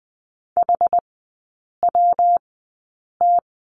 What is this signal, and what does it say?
Morse code, heard as a keyed tone, about 20 wpm, spaced as for 8 wpm, 700 Hz. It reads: HWT
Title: 133 Lesson 22e - Question Mark - Three Characters 20wpm